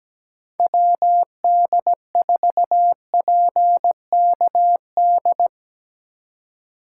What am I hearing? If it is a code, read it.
Morse code, heard as a keyed tone, about 20 wpm, 700 Hz. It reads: WD4PKD
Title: Speed Racing - USA Call Signs 17wpm